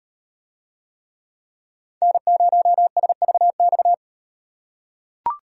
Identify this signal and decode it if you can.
Morse code, heard as a keyed tone, about 38 wpm, 700 Hz. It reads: N0SVX